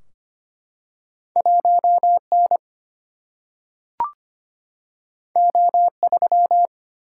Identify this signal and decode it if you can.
Morse code, heard as a keyed tone, about 25 wpm, 700 Hz. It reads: 1 N O 3
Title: Single-Letter Single-Number 25wpm